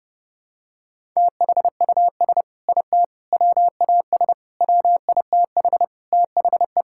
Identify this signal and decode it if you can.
Morse code, heard as a keyed tone, about 30 wpm, 700 Hz. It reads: THUS IT WAS WITH THE
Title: Sentences from Top 1000 Words - Review Part 5 - 30wpm